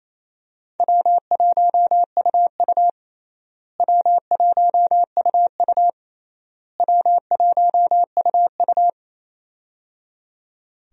Morse code, heard as a keyed tone, about 28 wpm, 700 Hz. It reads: W1UU W1UU W1UU
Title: USA Call Signs - 4 Characters - Repeated 3x - 28wpm